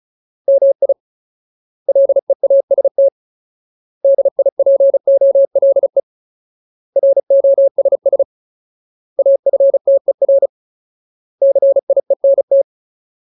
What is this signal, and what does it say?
Morse code, heard as a keyed tone, about 34 wpm, 550 Hz. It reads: MI LEAST DIPOLE ROSS AFTER CIENT
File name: Ultimate Word List / 35WPM Ultimate